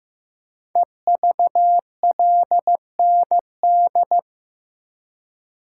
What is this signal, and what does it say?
Morse code, heard as a keyed tone, about 15 wpm, 700 Hz. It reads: EVLND
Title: Code Groups - 5 Letter:Numbers - 15wpm